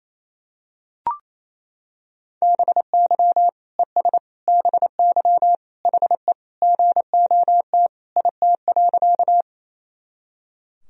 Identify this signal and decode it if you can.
Morse code, heard as a keyed tone, about 28 wpm, 700 Hz. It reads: BY ES BY HE GOT IT.
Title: Sentences from Top 300 Words and Abbreviations 28wpm